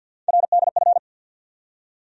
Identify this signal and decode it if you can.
Morse code, heard as a keyed tone, about 49 wpm, 700 Hz. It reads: RDF
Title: USA Call Sign Postfixes - 3 Letters 50wpm